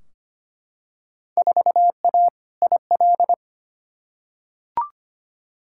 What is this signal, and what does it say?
Morse code, heard as a keyed tone, about 25 wpm, 700 Hz. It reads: 4A IL
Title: Field Day Exchanges - 25wpm